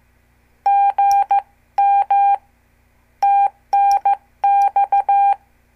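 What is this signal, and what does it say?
Morse code, heard as a keyed tone, about 15 wpm, 800 Hz. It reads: GM TNX
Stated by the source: Example CW QSO between N2APB and N2CX from a CWTD Morse lesson.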